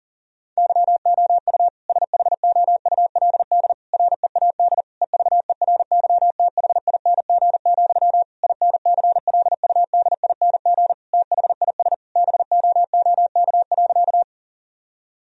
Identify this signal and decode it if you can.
Morse code, heard as a keyed tone, about 40 wpm, 700 Hz. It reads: YOU SHOULD READ EVERYTHING, INCLUDING THIS BOOK.
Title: Sentences from Top 1000 Words - Review Part 4 - 40wpm